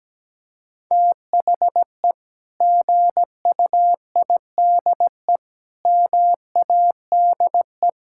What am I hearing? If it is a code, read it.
Morse code, heard as a keyed tone, about 17 wpm, 700 Hz. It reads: THE GUIDE MADE
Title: Sentences from Top 1000 Words - Review Part 6 - 17wpm